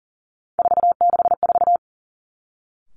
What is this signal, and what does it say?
Morse code, heard as a keyed tone, about 37 wpm, 700 Hz. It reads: 464
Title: Three Challenging Letter-Number Combinations 40wpm